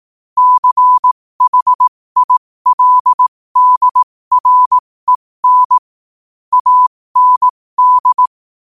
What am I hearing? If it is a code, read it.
Morse code, heard as a keyed tone, about 18 wpm, spaced as for 15 wpm, 1000 Hz. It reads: CHILDREN AND
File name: 15wpm